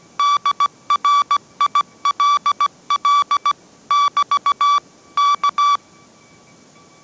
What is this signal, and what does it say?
Morse code, heard as a keyed tone, about 17 wpm, 1150 Hz. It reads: DRILL <BT> K